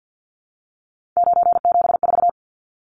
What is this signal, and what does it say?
Morse code, heard as a keyed tone, about 50 wpm, 700 Hz. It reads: 974
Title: Numbers - 3-Digit 50wpm